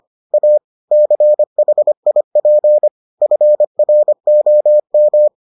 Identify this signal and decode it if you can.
Morse code, heard as a keyed tone, about 25 wpm, 600 Hz. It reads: A CHIP FROM